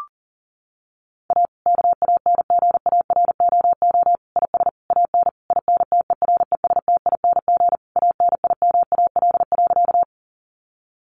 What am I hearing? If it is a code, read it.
Morse code, heard as a keyed tone, about 40 wpm, 700 Hz. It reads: A KANGAROO IS AN INTERESTING ANIMAL.
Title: Easy Phrases 40wpm